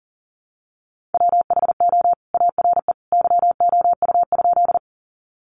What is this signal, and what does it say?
Morse code, heard as a keyed tone, about 40 wpm, 700 Hz. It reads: WHO ARE YOU?
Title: Sentences from Top 100 Words 40wpm